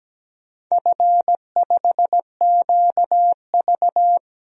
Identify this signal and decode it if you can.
Morse code, heard as a keyed tone, about 17 wpm, 700 Hz. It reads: F5QV
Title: DXCC-100 Call Signs - 4 Characters - 17wpm